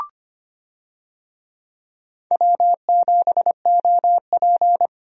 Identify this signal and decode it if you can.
Morse code, heard as a keyed tone, about 25 wpm, 700 Hz. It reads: W7OP
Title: USA Call Signs - 4 Characters - Repeated 2x - 25wpm